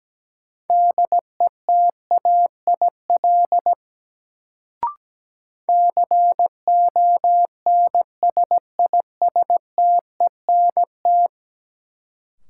Morse code, heard as a keyed tone, about 17 wpm, 700 Hz. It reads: DETAIL CONSISTENT
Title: Sentences from Top 1500 Words - New Word Drill Alt - 17wpm